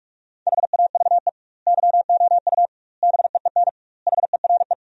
Morse code, heard as a keyed tone, about 41 wpm, 700 Hz. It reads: HAVE YOU BEEN HERE
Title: 2x Word-Spacing - Sentences from Top 200 Words - 45wpm